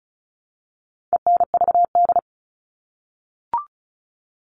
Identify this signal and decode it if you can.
Morse code, heard as a keyed tone, about 34 wpm, 700 Hz. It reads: ENVD
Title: All Cut Numbers - 4-Digit 35wpm